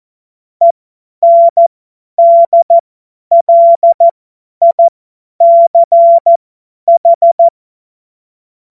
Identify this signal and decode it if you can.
Morse code, heard as a keyed tone, about 14 wpm, spaced as for 10 wpm, 680 Hz. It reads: ENDLICH
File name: Morsekurs / Witze / Witz 10. Januar 2023 50 F70